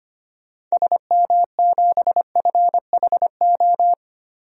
Morse code, heard as a keyed tone, about 25 wpm, 700 Hz. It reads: SM7FHO
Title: DXCC-100 Call Signs - 6 Characters - Repeated 3x - 25wpm